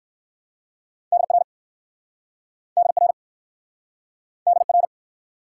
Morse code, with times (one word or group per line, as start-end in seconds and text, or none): 1.1-1.4 DR
2.8-3.1 DR
4.5-4.9 DR